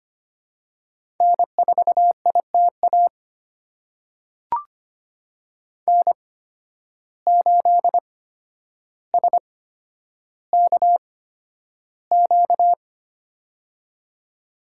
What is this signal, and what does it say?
Morse code, heard as a keyed tone, about 25 wpm, spaced as for 11 wpm, 700 Hz. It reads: N4ITA N8SKQ
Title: USA Call Signs - 5 Characters - 8x Character Spacing - 25wpm